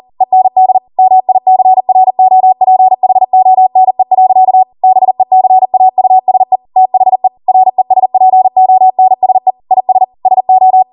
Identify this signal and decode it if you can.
Morse code, heard as a keyed tone, about 40 wpm, 750 Hz. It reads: END MICROPHONE. BECAUSE THE RESPONSE IS SO